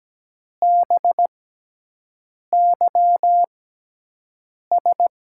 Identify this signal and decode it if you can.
Morse code, heard as a keyed tone, about 17 wpm, spaced as for 6 wpm, 700 Hz. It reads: BYS
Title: USA Call Signs - 6 Characters - 6x Character Spacing - 17wpm